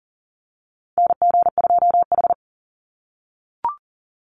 Morse code, heard as a keyed tone, about 38 wpm, 700 Hz. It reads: NG2H
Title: Call Signs - 40wpm